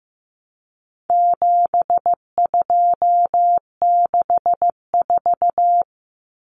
Morse code, heard as a keyed tone, about 15 wpm, 700 Hz. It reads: 7264
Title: Numbers - 4-Digit 15wpm